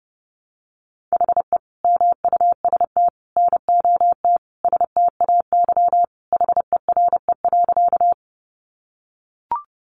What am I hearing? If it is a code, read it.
Morse code, heard as a keyed tone, about 30 wpm, 700 Hz. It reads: HE MUST NOT STAY HERE.
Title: Sentences from Top 600 Words 30wpm